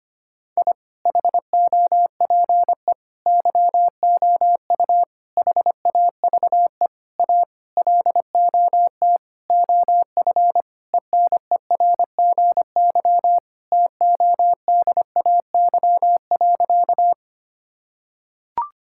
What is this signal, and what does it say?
Morse code, heard as a keyed tone, about 25 wpm, 700 Hz. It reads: I HOPE YOU HAVE A LOT OF ENERGY TODAY.
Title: Sentences from Top 1000 Words - Review Part 6 - 25wpm